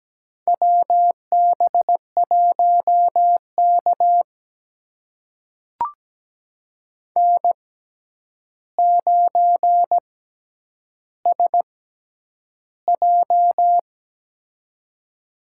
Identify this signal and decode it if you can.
Morse code, heard as a keyed tone, about 17 wpm, spaced as for 9 wpm, 700 Hz. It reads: WB1K N9SJ
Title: USA Call Signs - 4 Characters - 6x Character Spacing - 17wpm